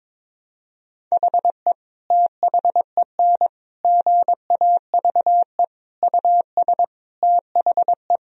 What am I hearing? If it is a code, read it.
Morse code, heard as a keyed tone, about 22 wpm, 700 Hz. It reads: HE THEN GAVE US THE
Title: Sentences from Top 1000 Words - Review Part 4 - 22wpm